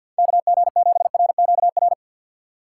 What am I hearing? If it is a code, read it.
Morse code, heard as a keyed tone, about 48 wpm, 700 Hz. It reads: KG7RQR
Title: USA Call Signs - 6 Characters - 6x Character Spacing - 50wpm